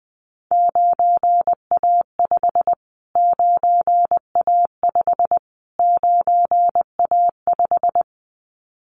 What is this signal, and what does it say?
Morse code, heard as a keyed tone, about 19 wpm, 700 Hz. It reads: 9A5 9A5 9A5
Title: Callsign Prefixes for Top 25 DXCC - Repeated 3x - 20wpm